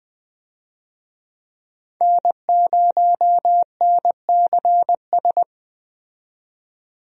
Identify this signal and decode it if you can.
Morse code, heard as a keyed tone, about 20 wpm, 700 Hz. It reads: N0NCS
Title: USA Call Signs - 5 Characters - 20wpm